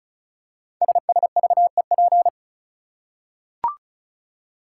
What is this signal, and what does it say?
Morse code, heard as a keyed tone, about 36 wpm, 700 Hz. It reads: SSVEP
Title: Code Groups - 5 Letters - 35wpm